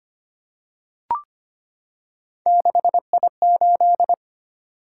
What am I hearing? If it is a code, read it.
Morse code, heard as a keyed tone, about 25 wpm, 700 Hz. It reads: 6I8
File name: Three Letter-Number Combinations - 25wpm